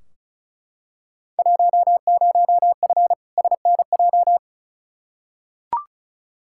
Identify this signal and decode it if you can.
Morse code, heard as a keyed tone, about 36 wpm, 690 Hz. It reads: 10F SNJ
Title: Field Day Exchanges Encore - 35wpm